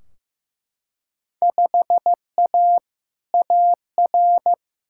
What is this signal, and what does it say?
Morse code, heard as a keyed tone, about 15 wpm, 700 Hz. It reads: 5A AR
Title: Field Day Exchanges Encore - 15wpm